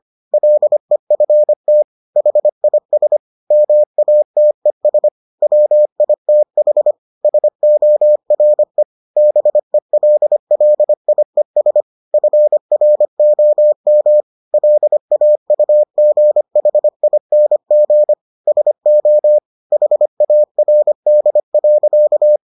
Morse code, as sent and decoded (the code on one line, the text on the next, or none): .-.. ..-. - / .... .. ... / -- .- - . ... / .-- .. - .... / ... --- .-. / -... .-.. .-.. .. ... / ..-. .-. --- -- / .-.. .- ..- --. .... .. -. --. / ... --- / .... .- .-. -.. .-.-.-
LEFT HIS MATES WITH SORE BELLIES FROM LAUGHING SO HARD.